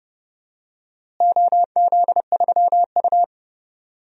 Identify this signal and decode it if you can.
Morse code, heard as a keyed tone, about 30 wpm, 700 Hz. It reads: OZ3U
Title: DXCC-100 Call Signs - 4 Characters - Repeated 3x - 30wpm